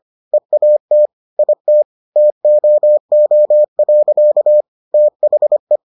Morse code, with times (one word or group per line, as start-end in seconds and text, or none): 0.3-1.1 EAT
1.4-1.8 IT
2.2-4.6 TOO.
4.9-5.8 THE